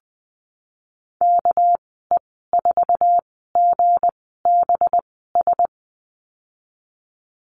Morse code, heard as a keyed tone, about 20 wpm, spaced as for 15 wpm, 700 Hz. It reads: KE4GBS
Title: 207 Lesson 33g - Letter j - Call Signs 20wpm